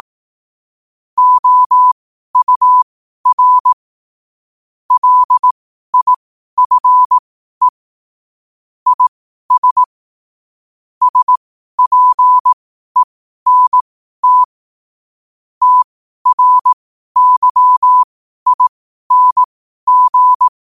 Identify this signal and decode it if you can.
Morse code, heard as a keyed tone, about 18 wpm, spaced as for 12 wpm, 1000 Hz. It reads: OUR LIFE IS SPENT TRYING
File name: qotdfiles / 12wpm